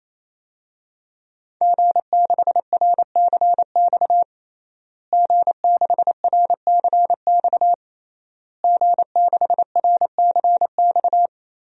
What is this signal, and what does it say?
Morse code, heard as a keyed tone, about 28 wpm, 700 Hz. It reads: G6RCX G6RCX G6RCX
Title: DXCC-100 Call Signs - 5 Characters - Repeated 3x - 28wpm